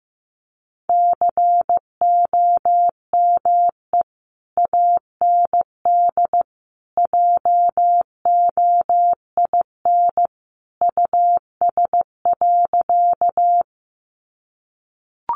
Morse code, heard as a keyed tone, about 15 wpm, 700 Hz. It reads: COME AND JOIN US.
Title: Sentences from Top 1400 Words 15wpm